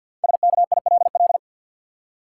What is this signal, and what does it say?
Morse code, heard as a keyed tone, about 53 wpm, 700 Hz. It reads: SKILL
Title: Five Letter Words - 50wpm